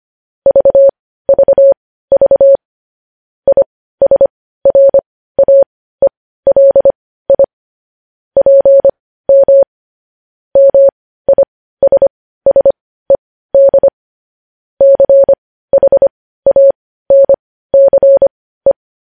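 Morse code, heard as a keyed tone, about 25 wpm, spaced as for 15 wpm, 550 Hz. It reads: VVV ISRAELI PM MISSED CHANCE
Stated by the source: Morse code news headlines at 15 WPM.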